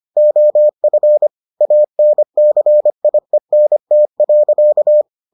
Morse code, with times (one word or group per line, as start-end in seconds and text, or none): 0.2-1.3 OF
1.6-5.0 ANCIENT.